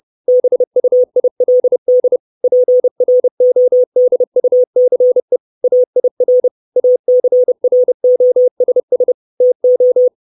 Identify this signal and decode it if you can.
Morse code, heard as a keyed tone, about 30 wpm, 500 Hz. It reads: BUILD PRODUCE AIR ACROSS TO
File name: 632 Words 30WPM